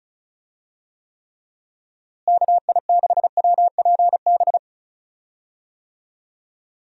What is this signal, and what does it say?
Morse code, heard as a keyed tone, about 36 wpm, 700 Hz. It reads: KI6WPB
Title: USA Call Signs - 6 Characters - Repeated 3x - 35wpm